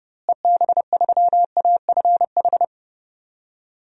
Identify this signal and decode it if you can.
Morse code, heard as a keyed tone, about 30 wpm, 700 Hz. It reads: EB3AFH
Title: DXCC-100 Call Signs - 6 Characters - Repeated 2x - 30wpm